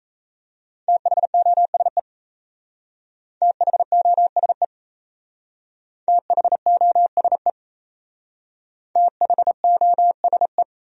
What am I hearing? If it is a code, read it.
Morse code, heard as a keyed tone, about 37 wpm, 700 Hz. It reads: THOSE THOSE THOSE THOSE